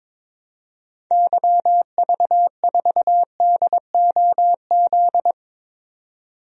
Morse code, heard as a keyed tone, about 22 wpm, 700 Hz. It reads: YV4DOZ
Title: DXCC-100 Call Signs - 6 Characters - 22wpm